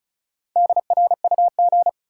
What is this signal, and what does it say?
Morse code, heard as a keyed tone, about 36 wpm, 700 Hz. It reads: DRUG
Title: Sets of 5 Words - New Word Drill - 35wpm